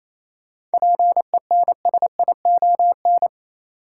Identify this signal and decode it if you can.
Morse code, heard as a keyed tone, about 28 wpm, 700 Hz. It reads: PENSION
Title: Seven Letter Words - 28wpm